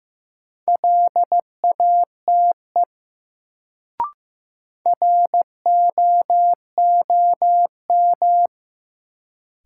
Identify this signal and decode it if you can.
Morse code, heard as a keyed tone, about 15 wpm, 700 Hz. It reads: LATE ROOM
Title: Sentences from Top 300 Words - New Word Drill - 15wpm